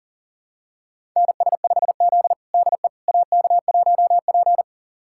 Morse code, heard as a keyed tone, about 37 wpm, 700 Hz. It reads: NS5Z DE AK1P
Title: USA Callsign DE Callsign - 4 Characters - 40wpm